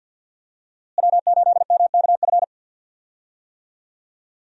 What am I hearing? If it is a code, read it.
Morse code, heard as a keyed tone, about 52 wpm, 700 Hz. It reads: W8MKF
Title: USA Call Signs - 5 Characters - Repeated 2x - 50wpm